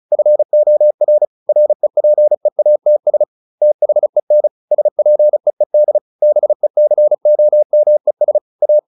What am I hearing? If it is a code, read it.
Morse code, heard as a keyed tone, about 34 wpm, 600 Hz. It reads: FOR REPEATS THEN SPEED BECOMES A